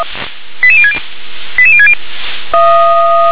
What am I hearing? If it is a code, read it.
Five-tone selective call (ZVEI), heard as quick runs of five tones, tones of about 70 ms. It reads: 88060 88060